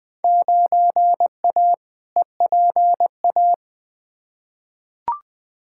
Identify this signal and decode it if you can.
Morse code, heard as a keyed tone, about 20 wpm, 700 Hz. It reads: 9A EPA
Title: Field Day Exchanges Encore - 20wpm